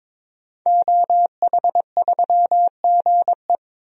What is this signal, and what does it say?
Morse code, heard as a keyed tone, about 22 wpm, 700 Hz. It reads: OH3GE
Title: DXCC-100 Call Signs - 5 Characters - 22wpm